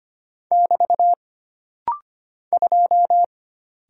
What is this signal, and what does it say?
Morse code, heard as a keyed tone, about 25 wpm, 700 Hz. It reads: <BT> 2
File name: Single Character (Letter:Number:Punctuation:Prosign) - Rapid-Fire - 25wpm